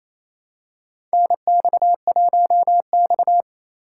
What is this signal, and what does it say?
Morse code, heard as a keyed tone, about 28 wpm, 700 Hz. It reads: NX1X
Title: USA Call Signs - 4 Characters - 2x Character Spacing - 28wpm